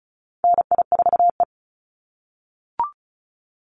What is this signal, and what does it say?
Morse code, heard as a keyed tone, about 34 wpm, 700 Hz. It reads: NI4E
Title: Call Signs - 35wpm